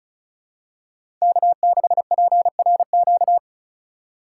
Morse code, heard as a keyed tone, about 34 wpm, 700 Hz. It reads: K6PRQ
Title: USA Call Signs - 5 Characters - 2x Character Spacing - 35wpm